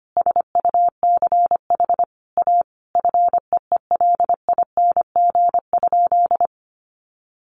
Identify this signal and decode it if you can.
Morse code, heard as a keyed tone, about 25 wpm, 700 Hz. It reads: SUCH A FEELING?
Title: Sentences from Top 1200 Words 25wpm